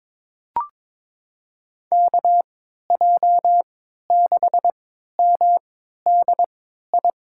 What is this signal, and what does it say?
Morse code, heard as a keyed tone, about 22 wpm, spaced as for 13 wpm, 700 Hz. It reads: KJ6MDI
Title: USA Call Signs - 6 Characters - 3x Character Spacing - 22wpm